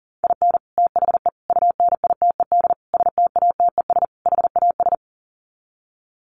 Morse code, heard as a keyed tone, about 40 wpm, 700 Hz. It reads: IN THE UNITED STATES HAS
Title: Sets of 5 Words Encore2 40wpm